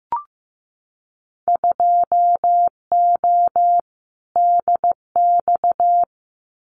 Morse code, heard as a keyed tone, about 15 wpm, 700 Hz. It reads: 2O DX
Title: Winter Field Day Exchanges Encore 15wpm